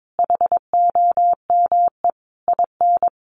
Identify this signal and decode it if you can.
Morse code, heard as a keyed tone, about 22 wpm, 700 Hz. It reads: HOME IN
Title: Sentences from Top 1500 Words 22wpm